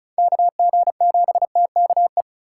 Morse code, heard as a keyed tone, about 46 wpm, 700 Hz. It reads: KG7TKE